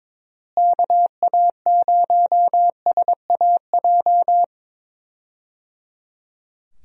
Morse code, heard as a keyed tone, about 22 wpm, 700 Hz. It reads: KA0SAJ
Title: USA Call Signs - 6 Characters - Repeated 3x - 22wpm